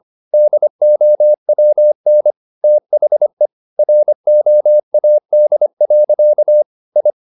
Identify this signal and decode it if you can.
Morse code, heard as a keyed tone, about 25 wpm, 600 Hz. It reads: DOWN THE ROAD. I
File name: the old man and the sea part 1 25wpm